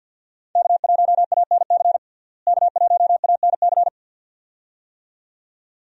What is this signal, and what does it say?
Morse code, heard as a keyed tone, about 46 wpm, 700 Hz. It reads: K1ANC K1ANC
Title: USA Call Signs - 5 Characters - Repeated 2x - 50wpm